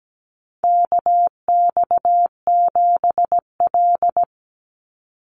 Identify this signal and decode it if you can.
Morse code, heard as a keyed tone, about 17 wpm, 700 Hz. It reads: KX7L
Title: NA SOTA Chaser Callsigns 17wpm